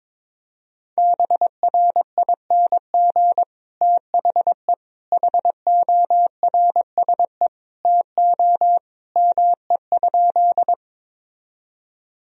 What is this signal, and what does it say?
Morse code, heard as a keyed tone, about 22 wpm, 700 Hz. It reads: BRING THE HORSE TO ME?
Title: Sentences from Top 1000 Words - Review Part 3 - 22wpm